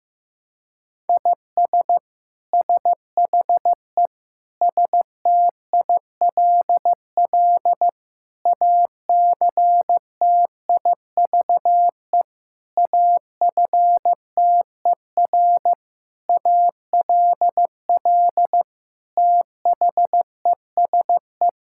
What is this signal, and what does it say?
Morse code, heard as a keyed tone, about 15 wpm, 700 Hz. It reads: IS SHE STILL ACTIVE AFTER ALL THESE
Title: Sentences from Top 1000 Words - Review Part 6 - 15wpm